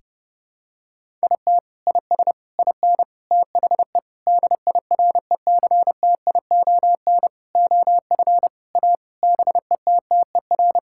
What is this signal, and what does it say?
Morse code, heard as a keyed tone, about 32 wpm, 700 Hz. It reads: IT IS IN THE DIRECTION OF A BETTER
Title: Sentences from Top 1000 Words - Review Part 5 - 30wpm